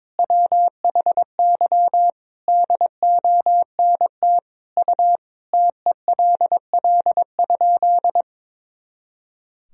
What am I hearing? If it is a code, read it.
Morse code, heard as a keyed tone, about 21 wpm, 700 Hz. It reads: WHY DONT U TELL?
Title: Sentences from Top 300 Words and Abbreviations 22wpm